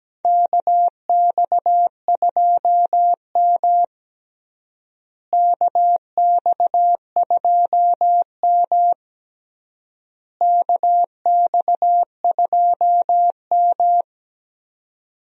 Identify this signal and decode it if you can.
Morse code, heard as a keyed tone, about 17 wpm, 700 Hz. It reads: KX2M KX2M KX2M
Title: USA Call Signs - 4 Characters - Repeated 3x - 17wpm